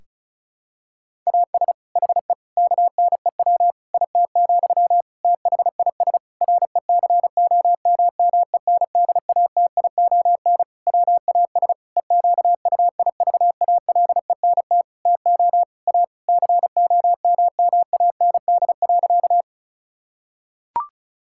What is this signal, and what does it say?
Morse code, heard as a keyed tone, about 36 wpm, 700 Hz. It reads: AS HE KNEW IT, THIS RECOMMENDATION WAS EQUIVALENT TO A COMMAND.
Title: Sentences from Top 1600 Words 35wpm